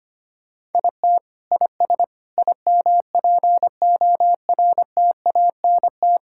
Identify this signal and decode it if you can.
Morse code, heard as a keyed tone, about 25 wpm, 700 Hz. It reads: IT IS IMPORTANT
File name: practice / 25wpm / Sentences with 4-Word Phrases - 25wpm